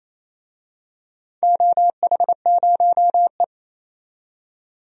Morse code, heard as a keyed tone, about 28 wpm, 700 Hz. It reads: OH0E
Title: DXCC-100 Call Signs - 4 Characters - 28wpm